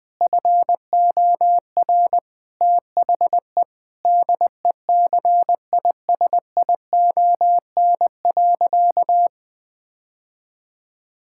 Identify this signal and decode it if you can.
Morse code, heard as a keyed tone, about 20 wpm, 700 Hz. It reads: FOR THE DECISION.
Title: Sentences from Top 1600 Words 20wpm